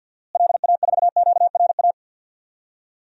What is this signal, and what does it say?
Morse code, heard as a keyed tone, about 47 wpm, 700 Hz. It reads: LA4QRA
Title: DXCC-100 Call Signs - 6 Characters - Repeated 3x - 50wpm